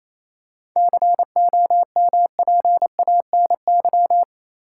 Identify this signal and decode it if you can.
Morse code, heard as a keyed tone, about 28 wpm, 700 Hz. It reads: COMPANY
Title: Seven Letter Words - 28wpm